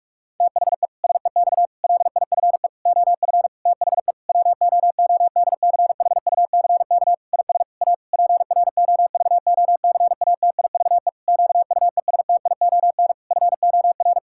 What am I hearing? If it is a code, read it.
Morse code, heard as a keyed tone, about 41 wpm, 700 Hz. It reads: THE SEX LIFE OF THE WOODCHUCK IS A PROVOCATIVE QUESTION FOR
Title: Pangram Challenge v2 - 45wpm